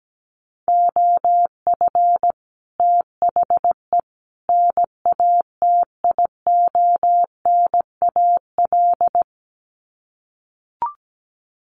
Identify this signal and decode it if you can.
Morse code, heard as a keyed tone, about 17 wpm, 700 Hz. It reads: OF THE NATIONAL
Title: Sets of 3 Words 17wpm